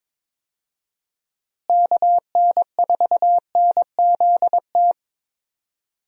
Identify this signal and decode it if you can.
Morse code, heard as a keyed tone, about 21 wpm, 700 Hz. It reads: KN4NZT
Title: USA Call Signs - 6 Characters - Repeated 3x - 22wpm